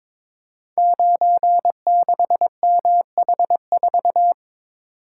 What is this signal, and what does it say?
Morse code, heard as a keyed tone, about 22 wpm, 690 Hz. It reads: 96MH4
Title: Code Groups - 5 Letter:Numbers - 22wpm